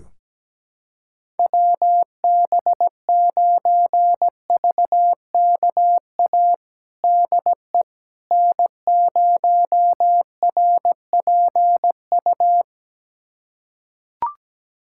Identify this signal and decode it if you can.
Morse code, heard as a keyed tone, about 17 wpm, 700 Hz. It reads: WB9VKA DE N0RPU
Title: USA Callsign DE Callsign 17wpm